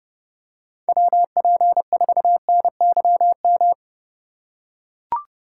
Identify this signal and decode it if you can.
Morse code, heard as a keyed tone, about 32 wpm, 700 Hz. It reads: WP4NYM